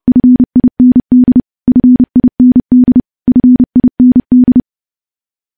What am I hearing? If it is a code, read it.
Morse code, heard as a keyed tone, about 30 wpm, 250 Hz. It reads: FIND FIND FIND
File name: morse / morse-training-30WPM